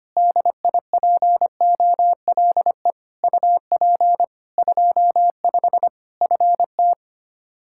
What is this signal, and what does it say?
Morse code, heard as a keyed tone, about 25 wpm, 700 Hz. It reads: DIPOLE UP 25 FT
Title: Practice QSO - Exchange 2 - Info - 25wpm